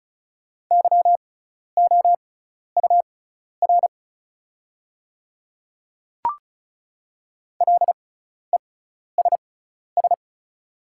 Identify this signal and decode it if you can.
Morse code, heard as a keyed tone, about 36 wpm, spaced as for 12 wpm, 700 Hz. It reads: YOUR LESS
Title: 6x Character Spacing - Top 500 Words - 35wpm